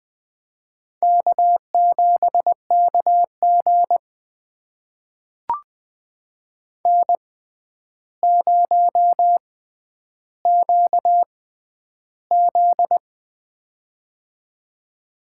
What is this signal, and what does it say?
Morse code, heard as a keyed tone, about 20 wpm, spaced as for 10 wpm, 700 Hz. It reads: K7KG N0QZ
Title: USA Call Signs - 4 Characters - 6x Character Spacing - 20wpm